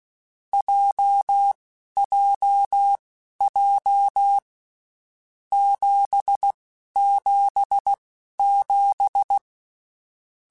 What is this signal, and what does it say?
Morse code, heard as a keyed tone, about 16 wpm, spaced as for 11 wpm, 800 Hz. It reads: JJJ 777
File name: Basic Sets / Basic Letter Groups Collective7 / JZ17@10